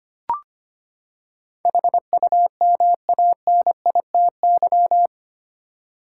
Morse code, heard as a keyed tone, about 25 wpm, 700 Hz. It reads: HUMANITY
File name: Eight Letter Words - 25wpm